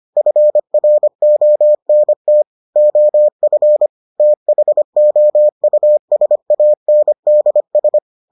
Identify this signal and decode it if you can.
Morse code, heard as a keyed tone, about 25 wpm, 600 Hz. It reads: FRONT OF THOUSANDS